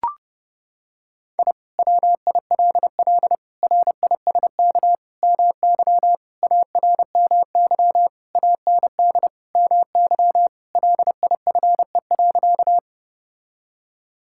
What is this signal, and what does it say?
Morse code, heard as a keyed tone, about 30 wpm, 700 Hz. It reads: I WILL RISK MY ARMY AND MY LIFE.
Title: Sentences from Top 1000 Words - Review Part 5 - 30wpm